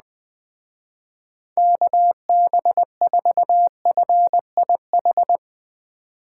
Morse code, heard as a keyed tone, about 19 wpm, 700 Hz. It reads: KB4FIH
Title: Speed Racing - USA 2x3 Call Signs 17wpm